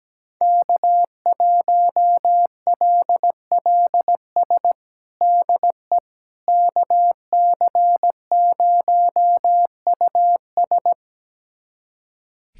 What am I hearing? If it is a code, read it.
Morse code, heard as a keyed tone, about 17 wpm, 700 Hz. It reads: K1LLS DE KC0US
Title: USA Callsign DE Callsign - 5 Characters - 17wpm